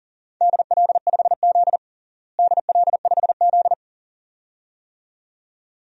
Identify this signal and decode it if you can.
Morse code, heard as a keyed tone, about 40 wpm, 690 Hz. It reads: DL5Z DL5Z
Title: DXCC-100 Call Signs - 4 Characters - Repeated 2x - 40wpm